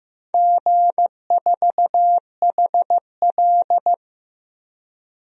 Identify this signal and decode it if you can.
Morse code, heard as a keyed tone, about 15 wpm, 700 Hz. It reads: G4HL
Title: DXCC-100 Call Signs - 4 Characters - 15wpm